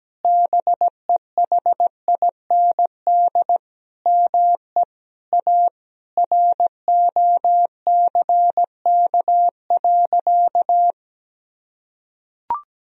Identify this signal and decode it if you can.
Morse code, heard as a keyed tone, about 17 wpm, 700 Hz. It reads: BEHIND ME A ROCK.
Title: Sentences from Top 1000 Words - Review Part 2 - 17wpm